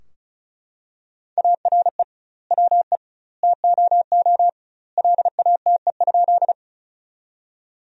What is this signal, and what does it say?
Morse code, heard as a keyed tone, about 34 wpm, 700 Hz. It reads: ARE WE TOO LATE?
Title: 2x Word-Spacing - Sentences from Top 300 Words - 35wpm